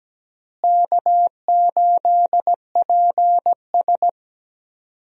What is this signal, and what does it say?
Morse code, heard as a keyed tone, about 17 wpm, 700 Hz. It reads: K8PS
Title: USA Call Signs - 4 Characters - Repeated 3x - 17wpm